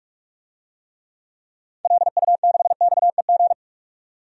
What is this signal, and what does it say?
Morse code, heard as a keyed tone, about 46 wpm, 700 Hz. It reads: LU6XEG
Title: DXCC-100 Call Signs - 6 Characters - 45wpm